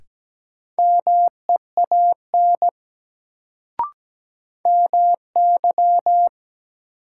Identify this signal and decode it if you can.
Morse code, heard as a keyed tone, about 17 wpm, 700 Hz. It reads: MEAN MY
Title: Sentences from Top 200 Words - New Word Drill - 17wpm